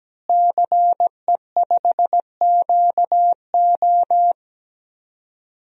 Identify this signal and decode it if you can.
Morse code, heard as a keyed tone, about 17 wpm, 700 Hz. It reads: CE5QO
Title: DXCC-100 Call Signs - 5 Characters - Repeated 2x - 17wpm